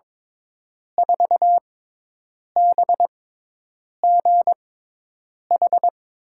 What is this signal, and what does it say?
Morse code, heard as a keyed tone, about 21 wpm, spaced as for 8 wpm, 700 Hz. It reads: 4BGH
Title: USA Call Signs - 6 Characters - 6x Character Spacing - 22wpm